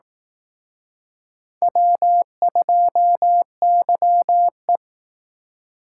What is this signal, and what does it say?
Morse code, heard as a keyed tone, about 19 wpm, 700 Hz. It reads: W2YE